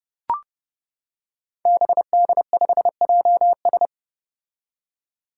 Morse code, heard as a keyed tone, about 31 wpm, 700 Hz. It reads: BD5JS
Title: Five Challenging Letter-Number Combinations 30wpm